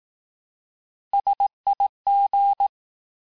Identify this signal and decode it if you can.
Morse code, heard as a keyed tone, about 18 wpm, 800 Hz. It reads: SIG
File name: cours-du-lundi files / cours-1852-abreviations-vitesse-17MPM-puis-22MPM